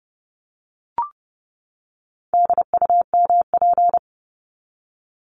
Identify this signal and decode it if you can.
Morse code, heard as a keyed tone, about 29 wpm, 700 Hz. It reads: DUMP
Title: Top 1001-1250 Words 30wpm